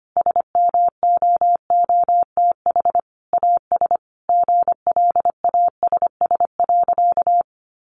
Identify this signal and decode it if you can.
Morse code, heard as a keyed tone, about 25 wpm, 700 Hz. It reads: SMOOTH AS GLASS.
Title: Sentences from Top 1400 Words 25wpm